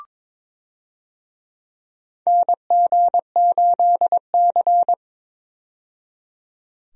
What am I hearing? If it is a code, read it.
Morse code, heard as a keyed tone, about 21 wpm, 700 Hz. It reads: NG8C
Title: USA Call Signs - 4 Characters - 22wpm